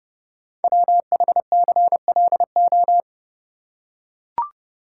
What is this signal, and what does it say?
Morse code, heard as a keyed tone, about 30 wpm, 700 Hz. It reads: WHCLO
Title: Code Groups - 5 Letter:Numbers - 30wpm